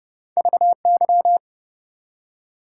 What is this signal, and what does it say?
Morse code, heard as a keyed tone, about 30 wpm, 700 Hz. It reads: VY
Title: Top 90 QSO Elements 30wpm